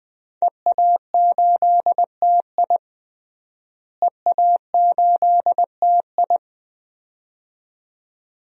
Morse code, heard as a keyed tone, about 20 wpm, 700 Hz. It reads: EA8TI EA8TI
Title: DXCC-100 Call Signs - 5 Characters - Repeated 2x - 20wpm